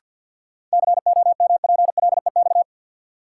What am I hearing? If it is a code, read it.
Morse code, heard as a keyed tone, about 53 wpm, 700 Hz. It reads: COMPLEX
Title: Sentences from Top 900 Words - New Word Drill - 50wpm